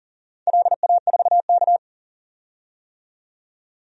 Morse code, heard as a keyed tone, about 39 wpm, 700 Hz. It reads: LA4K
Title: DXCC-100 Call Signs - 4 Characters - 40wpm